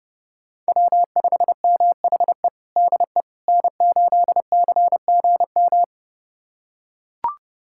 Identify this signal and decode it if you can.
Morse code, heard as a keyed tone, about 31 wpm, 700 Hz. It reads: W5MHE DE N8CGM
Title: USA Callsign DE Callsign - 5 Characters - 30wpm